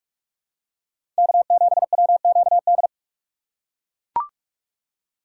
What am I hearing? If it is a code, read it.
Morse code, heard as a keyed tone, about 41 wpm, 700 Hz. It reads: K7WQD
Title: USA Call Signs - 5 Characters - 2x Character Spacing - 45wpm